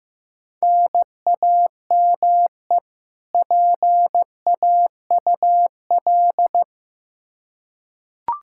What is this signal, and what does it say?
Morse code, heard as a keyed tone, about 15 wpm, 700 Hz. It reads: NAME PAUL
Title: Practice QSO - Exchange 1 - Beginning - 15wpm